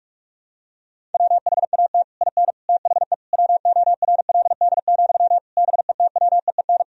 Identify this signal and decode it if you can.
Morse code, heard as a keyed tone, about 46 wpm, 700 Hz. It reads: WHAT IN THE WORLD, BETWEEN
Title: Sentences from Top 1000 Words - Review Part 1 - 45wpm